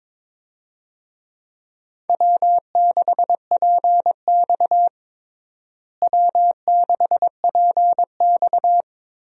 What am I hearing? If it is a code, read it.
Morse code, heard as a keyed tone, about 22 wpm, 700 Hz. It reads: W6PX W6PX
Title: USA Call Signs - 4 Characters - Repeated 3x - 22wpm